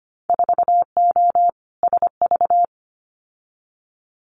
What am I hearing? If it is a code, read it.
Morse code, heard as a keyed tone, about 25 wpm, 700 Hz. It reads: 4O SV
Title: Winter Field Day Exchanges Encore 25wpm